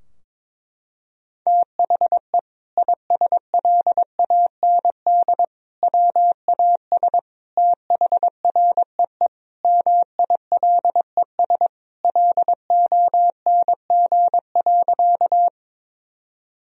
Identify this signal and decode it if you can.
Morse code, heard as a keyed tone, about 22 wpm, 700 Hz. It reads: THE ISLAND WAS THREE MILES LONG.
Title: Sentences from Top 1000 Words - Review Part 3 - 22wpm